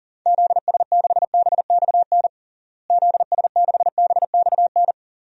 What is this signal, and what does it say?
Morse code, heard as a keyed tone, about 37 wpm, 700 Hz. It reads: ZS6BXN ZS6BXN
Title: DXCC-100 Call Signs - 6 Characters - Repeated 3x - 40wpm